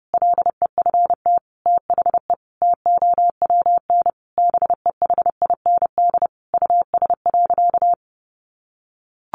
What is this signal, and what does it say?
Morse code, heard as a keyed tone, about 30 wpm, 700 Hz. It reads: LEFT THE TOWN BEHIND US.
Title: Sentences from Top 1300 Words 30wpm